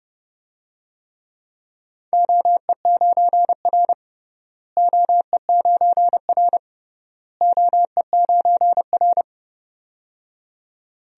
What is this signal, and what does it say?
Morse code, heard as a keyed tone, about 29 wpm, 700 Hz. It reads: OE9R OE9R OE9R
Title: DXCC-100 Call Signs - 4 Characters - Repeated 3x - 30wpm